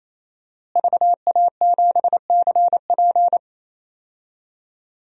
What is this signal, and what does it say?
Morse code, heard as a keyed tone, about 27 wpm, 700 Hz. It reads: VA7CP
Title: DXCC-100 Call Signs - 5 Characters - Repeated 3x - 28wpm